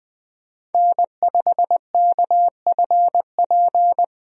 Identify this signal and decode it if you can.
Morse code, heard as a keyed tone, about 20 wpm, 700 Hz. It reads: N5KFP
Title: USA Call Signs - 5 Characters - 20wpm